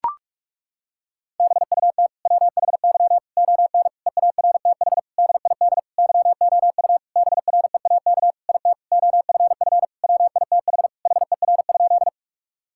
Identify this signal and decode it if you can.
Morse code, heard as a keyed tone, about 41 wpm, 700 Hz. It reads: BUT WHY ON EARTH DID YOU BREAK IT OFF WITH HER?
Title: Sentences from Top 1000 Words - Review Part 6 - 45wpm